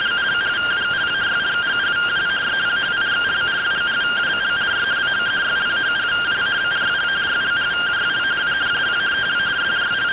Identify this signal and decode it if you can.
Two-tone teleprinter signal, about 170 Hz shift.